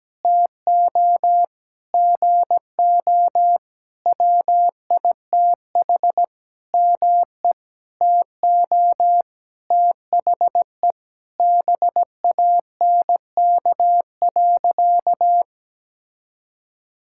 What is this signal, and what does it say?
Morse code, heard as a keyed tone, about 17 wpm, 700 Hz. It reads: TO GO WITH ME TO THE BANK.
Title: Sentences from Top 1000 Words - Review Part 6 - 17wpm